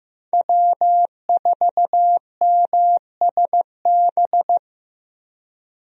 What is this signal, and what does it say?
Morse code, heard as a keyed tone, about 16 wpm, 700 Hz. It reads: W4MSB
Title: USA Call Signs - 5 Characters - Repeated 2x - 15wpm